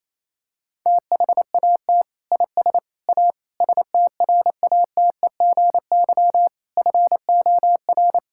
Morse code, heard as a keyed tone, about 27 wpm, 700 Hz. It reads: THAT IS A STRATEGY FOR
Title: Sentences from Top 1000 Words - Review Part 5 - 28wpm